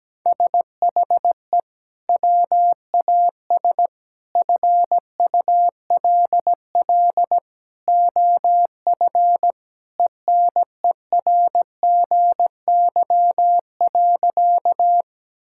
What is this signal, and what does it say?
Morse code, heard as a keyed tone, about 17 wpm, 700 Hz. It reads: SHE WAS FULL OF ENERGY.
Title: Sentences from Top 1000 Words - Review Part 6 - 17wpm